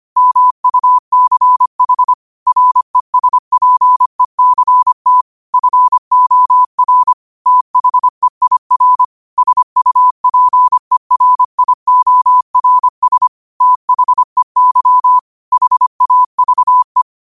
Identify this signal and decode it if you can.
Morse code, heard as a keyed tone, about 25 wpm, 1000 Hz. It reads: MUCH RESPECT FOR THEIR SUPERIORS THEY HAVE
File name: qotdfiles / 25wpm